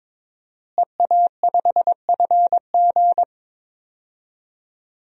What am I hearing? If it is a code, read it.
Morse code, heard as a keyed tone, about 22 wpm, 700 Hz. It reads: EA5FG
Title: DXCC-100 Call Signs - 5 Characters - Repeated 2x - 22wpm